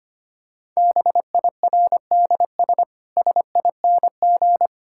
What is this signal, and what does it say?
Morse code, heard as a keyed tone, about 25 wpm, 700 Hz. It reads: BIRDS SING
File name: practice / 25wpm / Difficult Sentences with Hints - 25wpm